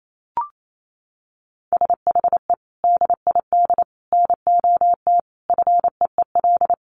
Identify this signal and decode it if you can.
Morse code, heard as a keyed tone, about 28 wpm, 700 Hz. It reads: SHE DID NOT FEEL
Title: Sentences from Top 1500 Words 28wpm